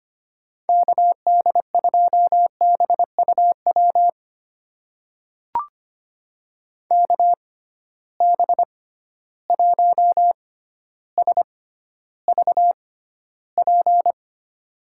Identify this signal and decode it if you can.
Morse code, heard as a keyed tone, about 25 wpm, spaced as for 13 wpm, 700 Hz. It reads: KD2BUW KB1SVP
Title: USA Call Signs - 6 Characters - 6x Character Spacing - 25wpm